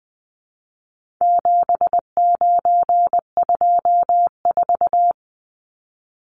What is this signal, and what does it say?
Morse code, heard as a keyed tone, about 20 wpm, 700 Hz. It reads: 7924
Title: Numbers - 4-Digit 20wpm